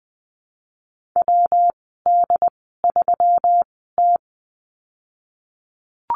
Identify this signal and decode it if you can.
Morse code, heard as a keyed tone, about 19 wpm, spaced as for 14 wpm, 700 Hz. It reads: WD3T
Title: 186 Lesson 30g - Letter b - Call Signs 20wpm